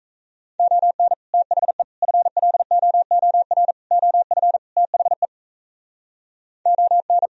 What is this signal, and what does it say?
Morse code, heard as a keyed tone, about 42 wpm, 700 Hz. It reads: ON THE FLOOR OF THE ON